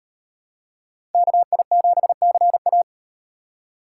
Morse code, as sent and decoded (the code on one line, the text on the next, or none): -.- .. --... -.-. .-
KI7CA